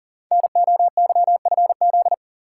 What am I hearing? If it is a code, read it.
Morse code, heard as a keyed tone, about 38 wpm, 700 Hz. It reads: NOYFZ
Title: Code Groups - 5 Letters - 40wpm